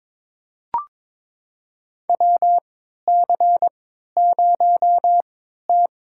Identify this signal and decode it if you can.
Morse code, heard as a keyed tone, about 21 wpm, spaced as for 12 wpm, 700 Hz. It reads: WC0T
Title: USA Call Signs - 3x Character Spacing - 22wpm